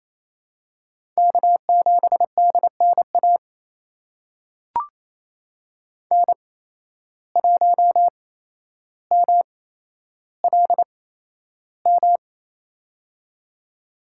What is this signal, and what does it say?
Morse code, heard as a keyed tone, about 28 wpm, spaced as for 12 wpm, 700 Hz. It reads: K7DNA N1MLM
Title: USA Call Signs - 5 Characters - 8x Character Spacing - 28wpm